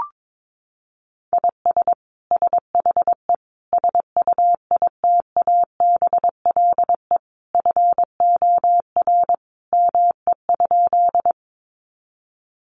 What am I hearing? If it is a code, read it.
Morse code, heard as a keyed tone, about 22 wpm, 700 Hz. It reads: IS SHE SUITABLE FOR ME?
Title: Sentences from Top 1400 Words 22wpm